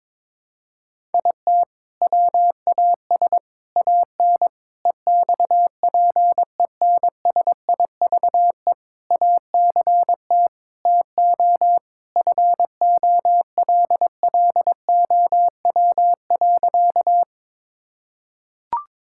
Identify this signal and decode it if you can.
Morse code, heard as a keyed tone, about 22 wpm, 700 Hz. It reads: IT WAS AN EXPENSIVE ACT TO FOLLOW.
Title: Sentences from Top 1000 Words - Review Part 6 - 22wpm